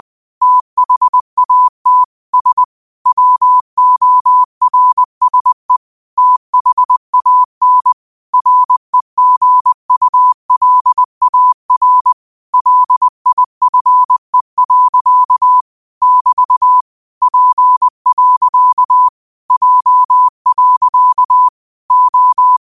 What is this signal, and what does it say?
Morse code, heard as a keyed tone, about 20 wpm, 1000 Hz. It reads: THAT S WORSE THAN REGULAR LIFE. <BT> P. J. O
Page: Quotes for Fri, 15 Aug 2025 in Morse Code at 20 words per minute.